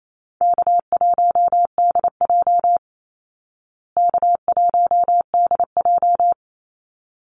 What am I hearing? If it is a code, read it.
Morse code, heard as a keyed tone, about 28 wpm, 700 Hz. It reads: K1DJ K1DJ
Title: Easy Call Signs Repeated 2x - 28wpm